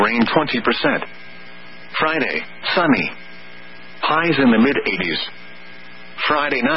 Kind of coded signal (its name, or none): Emergency Alert System (SAME) tones